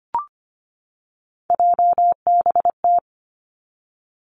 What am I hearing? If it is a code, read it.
Morse code, heard as a keyed tone, about 25 wpm, 700 Hz. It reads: JBT
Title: Three Challenging Letter-Number Combinations 25wpm